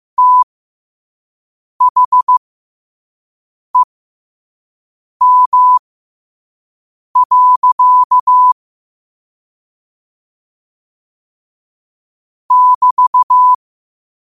Morse code, heard as a keyed tone, about 15 wpm, spaced as for 5 wpm, 1000 Hz. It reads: THEM. <BT>
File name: qotdfiles / 5wpm